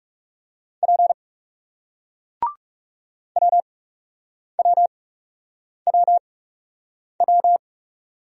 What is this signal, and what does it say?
Morse code, heard as a keyed tone, about 40 wpm, spaced as for 8 wpm, 700 Hz. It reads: P WWWW